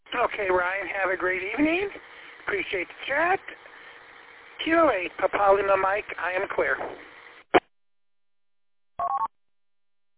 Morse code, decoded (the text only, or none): none